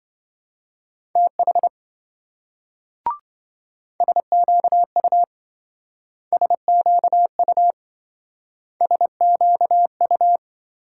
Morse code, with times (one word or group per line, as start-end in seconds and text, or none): 1.2-1.7 TH
4.0-5.2 SQU
6.3-7.7 SQU
8.8-10.4 SQU